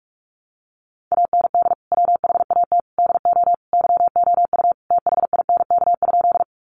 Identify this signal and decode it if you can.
Morse code, heard as a keyed tone, about 44 wpm, 700 Hz. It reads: AND WHAT DO YOU THINK?
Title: Sentences from Top 200 Words 45wpm